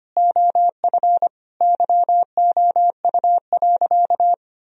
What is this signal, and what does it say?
Morse code, heard as a keyed tone, about 25 wpm, 700 Hz. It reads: OF YOU.
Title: Sentences from Top 1000 Words - Review Part 3 - 25wpm